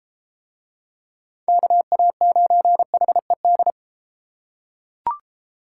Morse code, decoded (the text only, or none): KA9HED